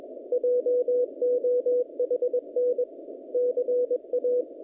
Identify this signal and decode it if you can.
Morse code, heard as a keyed tone, about 21 wpm, 490 Hz. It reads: JOHN CA
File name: contests / naqp-cw-k5tr-op-k5pi- -0000